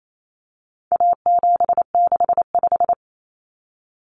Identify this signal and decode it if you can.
Morse code, heard as a keyed tone, about 28 wpm, 700 Hz. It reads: A765
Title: Common Cut Numbers - 4-Digit 28wpm